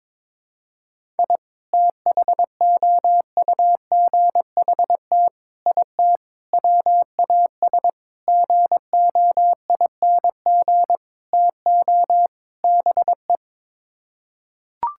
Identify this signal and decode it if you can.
Morse code, heard as a keyed tone, about 22 wpm, 700 Hz. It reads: I THOUGHT IT WAS GOING TO BE
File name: practice / 22wpm / Sets of 7 Words - 22wpm